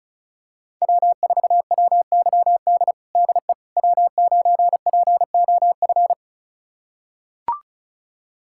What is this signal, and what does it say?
Morse code, heard as a keyed tone, about 36 wpm, 700 Hz. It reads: W4WYD DE W9POF
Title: USA Callsign DE Callsign - 5 Characters - 35wpm